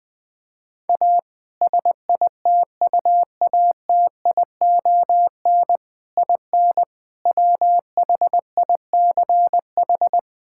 Morse code, heard as a keyed tone, about 21 wpm, 700 Hz. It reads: A SITUATION IN WHICH
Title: Sentences with 3-Word Phrases - 20wpm